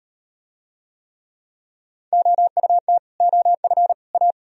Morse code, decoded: OUT OF A

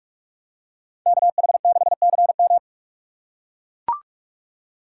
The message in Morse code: KH6CM

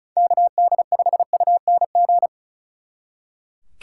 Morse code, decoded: KD5UNG

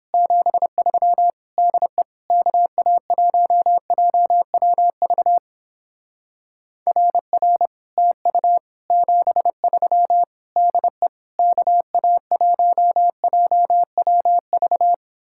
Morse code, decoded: 73 DE KA1JWV RR TU 73 DE KA1JWV

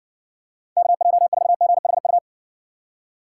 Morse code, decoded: KJ4GHU

